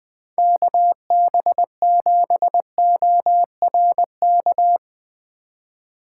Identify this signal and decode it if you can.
Morse code, heard as a keyed tone, about 20 wpm, 700 Hz. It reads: KB7ORK